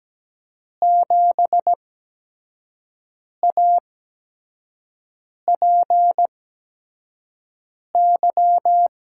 Morse code, heard as a keyed tone, about 17 wpm, spaced as for 5 wpm, 700 Hz. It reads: 7APY
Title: USA Call Signs - 8x Character Spacing - 17wpm